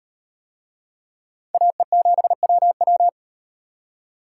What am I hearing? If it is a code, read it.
Morse code, heard as a keyed tone, about 37 wpm, 700 Hz. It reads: AE7WW